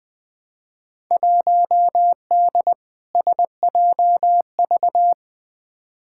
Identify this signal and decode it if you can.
Morse code, heard as a keyed tone, about 20 wpm, 700 Hz. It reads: 1D SJV
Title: Field Day Exchanges Encore - 20wpm